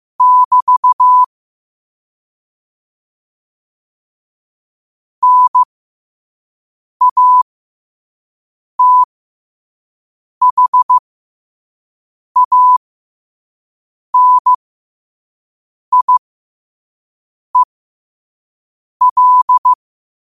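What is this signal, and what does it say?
Morse code, heard as a keyed tone, about 15 wpm, spaced as for 5 wpm, 1000 Hz. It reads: <BT> NATHANIEL
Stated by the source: Morse code sent at 5 wpm.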